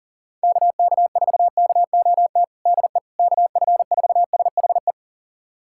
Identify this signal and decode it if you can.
Morse code, heard as a keyed tone, about 40 wpm, 700 Hz. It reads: KK4KOT DE KF4SHE
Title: USA Callsign DE Callsign 40wpm